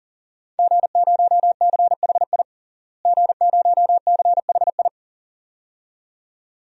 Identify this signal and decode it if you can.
Morse code, heard as a keyed tone, about 40 wpm, 700 Hz. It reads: G0CHI G0CHI
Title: DXCC-100 Call Signs - 5 Characters - Repeated 2x - 40wpm